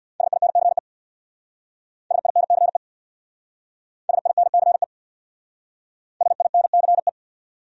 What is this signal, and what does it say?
Morse code, heard as a keyed tone, about 65 wpm, 700 Hz. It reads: SINCE SINCE SINCE SINCE